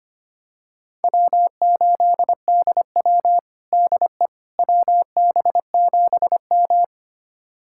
Morse code, heard as a keyed tone, about 25 wpm, 700 Hz. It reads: W8DW DE WB7M
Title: USA Callsign DE Callsign - 4 Characters - 25wpm